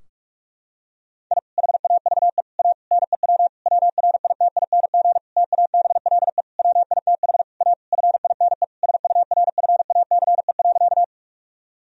Morse code, heard as a keyed tone, about 46 wpm, 700 Hz. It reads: I HAVE A NEW WRITING TABLE WITH A FINE SURFACE.
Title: Sentences from Top 1000 Words - Review Part 3 - 45wpm